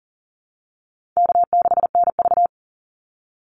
Morse code, heard as a keyed tone, about 40 wpm, 700 Hz. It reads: K6NV
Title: Easy Call Signs Repeated 3x - 40wpm